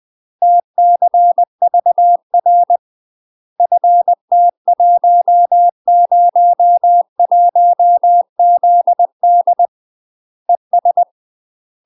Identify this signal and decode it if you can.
Morse code, heard as a keyed tone, about 20 wpm, 700 Hz. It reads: TCVR FT101ZD ES
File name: UK MASTER QSOs TRAINING / 20WPM / UK MASTER QSOs 0031